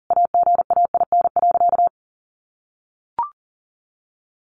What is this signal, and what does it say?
Morse code, heard as a keyed tone, about 37 wpm, 700 Hz. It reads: AGAIN.
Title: Sentences from Top 1100 Words 40wpm